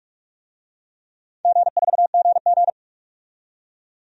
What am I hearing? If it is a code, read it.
Morse code, heard as a keyed tone, about 41 wpm, 700 Hz. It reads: G4GG
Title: DXCC-100 Call Signs - 4 Characters - 45wpm